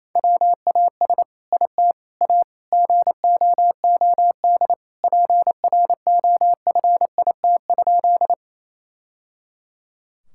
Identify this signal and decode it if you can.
Morse code, heard as a keyed tone, about 28 wpm, 700 Hz. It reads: WAS IT A GOOD PROFIT?
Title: Sentences from Top 1000 Words - Review Part 5 - 28wpm